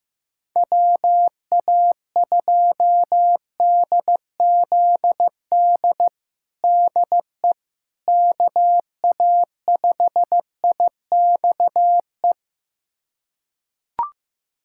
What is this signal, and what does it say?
Morse code, heard as a keyed tone, about 15 wpm, 700 Hz. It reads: WA2DZD DE KA5IXE
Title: USA Callsign DE Callsign - 6 Characters - 15wpm